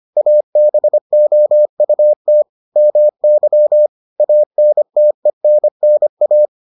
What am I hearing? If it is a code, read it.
Morse code, heard as a keyed tone, about 25 wpm, 600 Hz. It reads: ABOUT MY ANTENNA